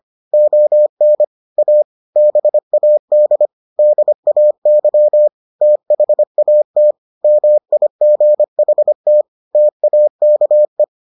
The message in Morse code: ON A BAD DAY THAT MIGHT TAKE

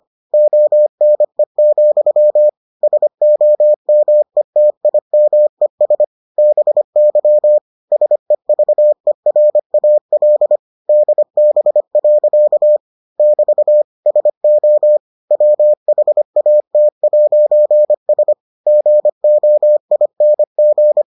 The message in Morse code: ONE, SOMETIMES BY SEVERAL DB. <BT> SO WHAT'S GOING

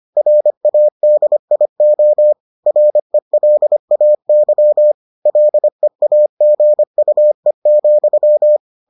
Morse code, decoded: RADIO RELAY LEAGUE,

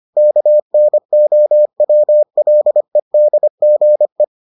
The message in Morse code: KNOWLEDGE

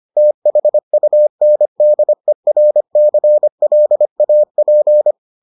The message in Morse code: THUNDERCLAP

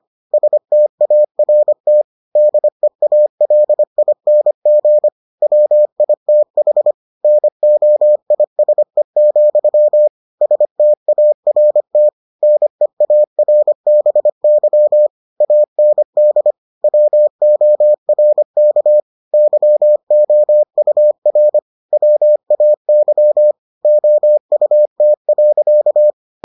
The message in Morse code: START DEALING WITH NOISE, START NEARBY AND WORK YOUR WAY OUT.

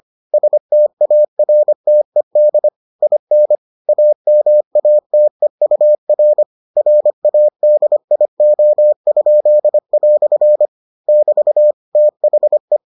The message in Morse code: STARTED IN AMATEUR RADIO?" <BT> THE

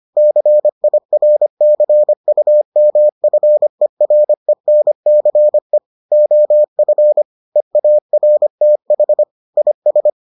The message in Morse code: CIRCUMFERENCE OF EARTH IS

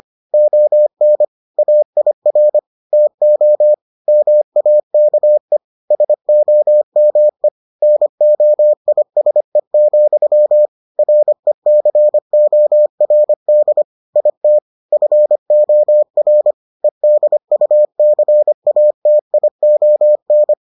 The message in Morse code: ON AIR TO MAKE SOME NOISE, RECORD IT FOR EDUCATION

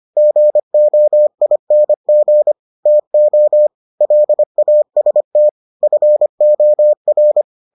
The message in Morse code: GOING TO LAST FOR

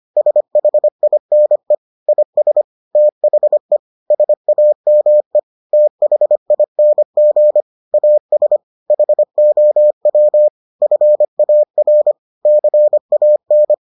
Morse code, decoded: SHINE IS THE SAME THING AS HOW FAR CAN

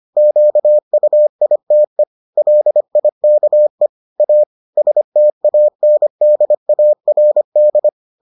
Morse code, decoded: QUITE LIKE A STANDARD

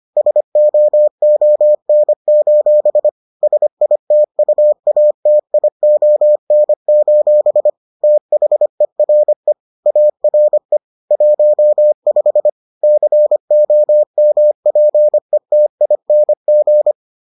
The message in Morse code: SOON: SITUATION: THERE ARE 15 COMPETING

